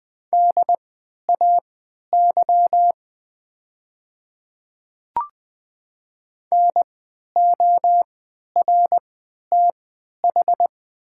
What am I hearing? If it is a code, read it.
Morse code, heard as a keyed tone, about 20 wpm, spaced as for 11 wpm, 700 Hz. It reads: DAY NORTH
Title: 3x Character Spacing - Top 300 Words - 20wpm